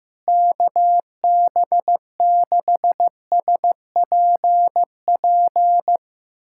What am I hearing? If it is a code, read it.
Morse code, heard as a keyed tone, about 15 wpm, 700 Hz. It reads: KB6SPP
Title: USA Callsign DE Callsign 15wpm